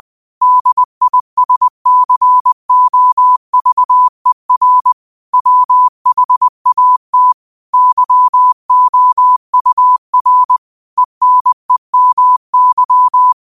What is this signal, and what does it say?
Morse code, heard as a keyed tone, about 20 wpm, 1000 Hz. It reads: DISCOVER WHAT YOUR ENEMY